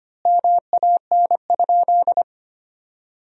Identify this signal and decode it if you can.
Morse code, heard as a keyed tone, about 25 wpm, 700 Hz. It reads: MAN?
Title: 3x Word-Spacing - Sentences from Top 400 Words - 25wpm